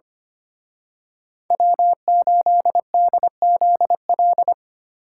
Morse code, heard as a keyed tone, about 28 wpm, 700 Hz. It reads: W8DZL
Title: Speed Racing - USA 1x3 Call Signs 25wpm